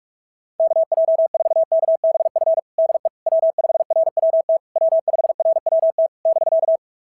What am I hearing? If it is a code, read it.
Morse code, heard as a keyed tone, about 46 wpm, 650 Hz. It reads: KJ4KBF DE W5RWT W5RWT <BK>
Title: SOTA QSO Practice 45wpm